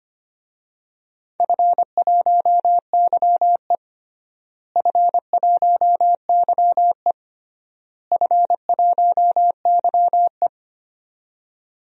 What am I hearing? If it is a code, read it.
Morse code, heard as a keyed tone, about 25 wpm, 700 Hz. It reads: F1YE F1YE F1YE
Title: DXCC-100 Call Signs - 4 Characters - Repeated 3x - 25wpm